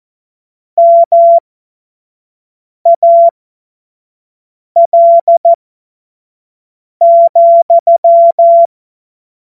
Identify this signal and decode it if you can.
Morse code, heard as a keyed tone, about 14 wpm, spaced as for 5 wpm, 680 Hz. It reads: MAL,
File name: Morsekurs / Witze / Witz 08. August 2023 25 F70